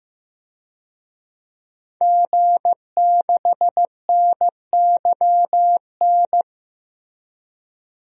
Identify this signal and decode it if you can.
Morse code, heard as a keyed tone, about 15 wpm, 700 Hz. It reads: G6NYN
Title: DXCC-100 Call Signs - 5 Characters - 15wpm